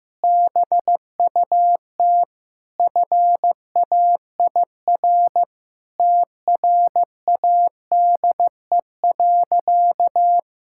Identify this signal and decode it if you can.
Morse code, heard as a keyed tone, about 16 wpm, 700 Hz. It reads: BUT FAIR TRADE.
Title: Sentences from Top 1600 Words 15wpm